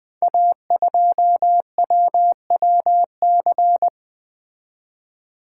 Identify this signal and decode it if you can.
Morse code, heard as a keyed tone, about 21 wpm, 700 Hz. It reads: A2WWC
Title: Code Groups - 5 Letter:Numbers - 20wpm